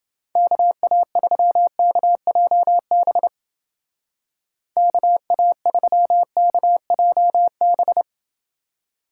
Morse code, heard as a keyed tone, about 28 wpm, 700 Hz. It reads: KA3KJB KA3KJB